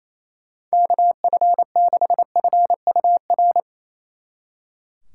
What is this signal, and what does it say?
Morse code, heard as a keyed tone, about 34 wpm, 700 Hz. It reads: KF6FUR
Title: Speed Racing - USA 2x3 Call Signs 28wpm